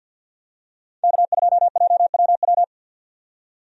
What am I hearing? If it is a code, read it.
Morse code, heard as a keyed tone, about 50 wpm, 700 Hz. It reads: K2JWW
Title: USA Call Signs - 5 Characters - 50wpm